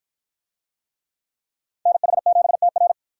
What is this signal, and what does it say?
Morse code, heard as a keyed tone, about 53 wpm, 700 Hz. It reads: NH7TR